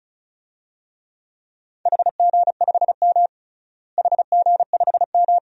Morse code, heard as a keyed tone, about 36 wpm, 700 Hz. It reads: HG5M HG5M
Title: DXCC-100 Call Signs - 4 Characters - Repeated 2x - 35wpm